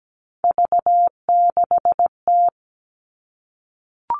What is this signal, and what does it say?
Morse code, heard as a keyed tone, about 17 wpm, 700 Hz. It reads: V6T
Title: All Cut Numbers - 3-Digit 17wpm